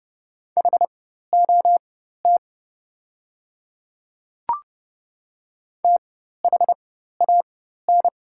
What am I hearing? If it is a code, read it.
Morse code, heard as a keyed tone, about 30 wpm, spaced as for 14 wpm, 700 Hz. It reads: HOT THAN
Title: 4x Character Spacing - Top 200 Words - 30wpm